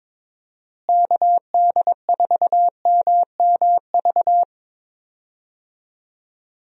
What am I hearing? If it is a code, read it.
Morse code, heard as a keyed tone, about 21 wpm, 700 Hz. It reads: KD4MMV
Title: USA Call Signs - 6 Characters - Repeated 3x - 22wpm